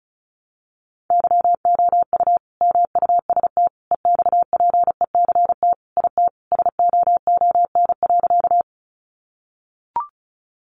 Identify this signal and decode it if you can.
Morse code, heard as a keyed tone, about 34 wpm, 700 Hz. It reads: YOU MUST EXPECT IT SOON.
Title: Sentences from Top 1100 Words 35wpm